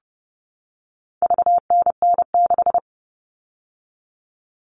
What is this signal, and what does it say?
Morse code, heard as a keyed tone, about 30 wpm, 700 Hz. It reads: VNN6
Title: All Cut Numbers - 4-Digit 30wpm